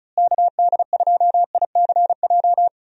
Morse code, heard as a keyed tone, about 36 wpm, 700 Hz. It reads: KD2ICJ
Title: USA Call Signs - 6 Characters - Repeated 3x - 35wpm